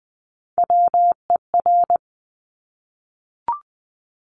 Morse code, heard as a keyed tone, about 20 wpm, 700 Hz. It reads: WER
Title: Top Three Letter Combinations in Words 20wpm